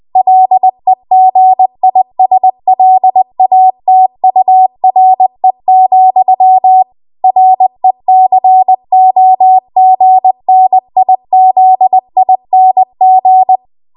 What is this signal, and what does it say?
Morse code, heard as a keyed tone, about 20 wpm, 750 Hz. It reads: LEGISLATURE, RECOGNIZING
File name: Morse / 20WPM